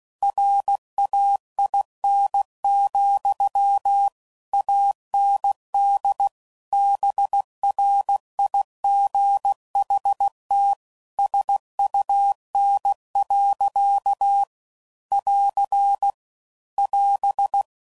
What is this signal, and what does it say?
Morse code, heard as a keyed tone, about 16 wpm, 800 Hz. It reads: RAIN, AND BRIGHT SUN. <AR> <AS>
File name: CW Code Groups @15WPM